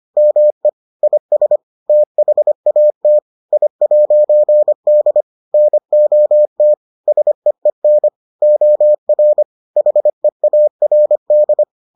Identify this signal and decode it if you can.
Morse code, heard as a keyed tone, about 25 wpm, 600 Hz. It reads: ME IS THAT I'D NOT SEEN OR HEARD